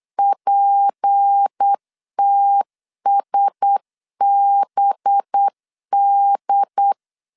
Morse code, heard as a keyed tone, about 8 wpm, 800 Hz. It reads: PTSBD
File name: LEZIONI / 6 PKX0 / 6 PKX 8wpm 3041